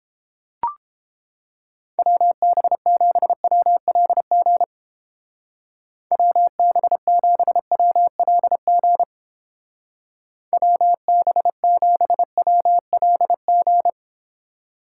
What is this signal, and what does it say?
Morse code, heard as a keyed tone, about 31 wpm, 700 Hz. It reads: WB7WLG WB7WLG WB7WLG